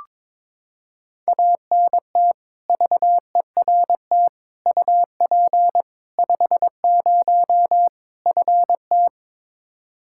Morse code, heard as a keyed tone, about 22 wpm, 700 Hz. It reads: ANT VERT UP 50 FT
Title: Practice QSO - Exchange 2 - Info - 22wpm